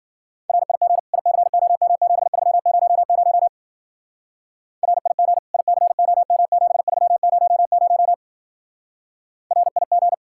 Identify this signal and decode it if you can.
Morse code, heard as a keyed tone, about 50 wpm, 700 Hz. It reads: RIG ICOM7300 RIG ICOM7300 RIG